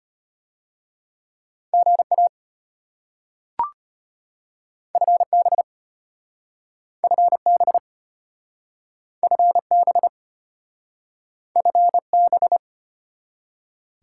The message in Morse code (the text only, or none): GA FB FB FB FB